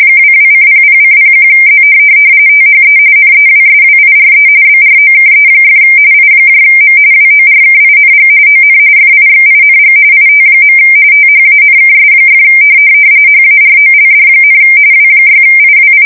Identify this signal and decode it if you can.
RTTY, about 45.45 baud. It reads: 22:00:01Z THE NEWSWIRE BROKE! IM WORKING ON FIXING IT. THANKS! HAROLD, W6IWI
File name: fsk170 / NYT-NEWS